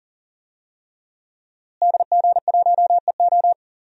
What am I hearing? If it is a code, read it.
Morse code, heard as a keyed tone, about 39 wpm, 700 Hz. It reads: DG1EO
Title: DXCC-100 Call Signs - 5 Characters - Repeated 3x - 40wpm